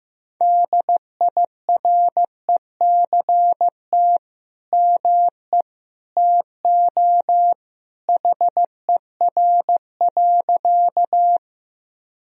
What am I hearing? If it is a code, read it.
Morse code, heard as a keyed tone, about 15 wpm, 700 Hz. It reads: DIRECT ME TO HER.
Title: Sentences from Top 1000 Words - Review Part 2 - 15wpm